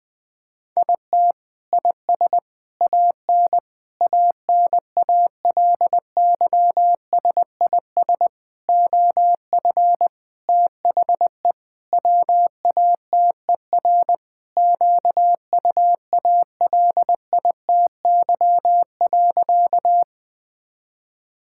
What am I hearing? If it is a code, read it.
Morse code, heard as a keyed tone, about 21 wpm, 700 Hz. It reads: IT IS AN ANALYSIS OF THE WATER QUALITY.
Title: Sentences from Top 1000 Words - Review Part 5 - 20wpm